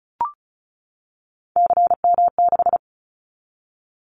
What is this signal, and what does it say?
Morse code, heard as a keyed tone, about 36 wpm, 700 Hz. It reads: CM6
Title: Callsign Prefixes for Top 100 DXCC - 35wpm